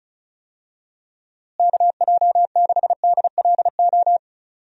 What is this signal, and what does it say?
Morse code, heard as a keyed tone, about 36 wpm, 700 Hz. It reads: KJ6DLO
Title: USA Call Signs - 6 Characters - Repeated 3x - 35wpm